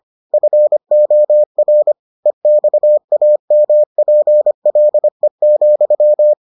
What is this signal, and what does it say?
Morse code, heard as a keyed tone, about 25 wpm, 600 Hz. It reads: FOR EXAMPLE,